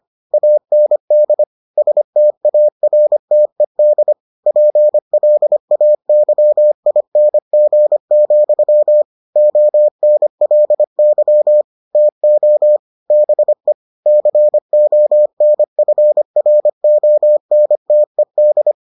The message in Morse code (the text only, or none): AND STARTED PLAYING, ONLY TO BE CONFRONTED